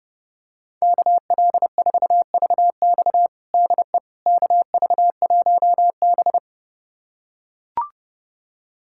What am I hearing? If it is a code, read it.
Morse code, heard as a keyed tone, about 30 wpm, 700 Hz. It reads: KL4VX DE KV1B
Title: USA Callsign DE Callsign 30wpm